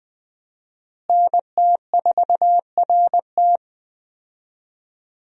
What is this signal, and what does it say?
Morse code, heard as a keyed tone, about 20 wpm, 700 Hz. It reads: NT4RT
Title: USA Call Signs - 5 Characters - Repeated 3x - 20wpm